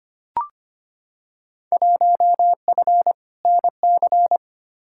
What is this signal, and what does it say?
Morse code, heard as a keyed tone, about 25 wpm, 700 Hz. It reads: 1F NC
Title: Field Day Exchanges - 25wpm